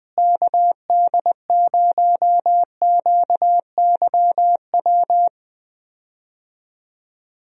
Morse code, heard as a keyed tone, about 20 wpm, 700 Hz. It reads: KD0QYW
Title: USA Call Signs - 6 Characters - Repeated 2x - 20wpm